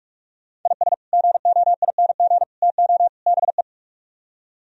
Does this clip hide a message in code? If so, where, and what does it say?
Yes: Morse code from 0.7 to 3.6 s, heard as a keyed tone, about 45 wpm, 700 Hz. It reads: IS GOING TO BE